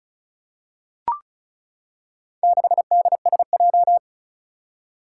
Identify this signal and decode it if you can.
Morse code, heard as a keyed tone, about 35 wpm, 700 Hz. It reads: 6DSJ
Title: Four Challenging Letter-Number Combinations 35wpm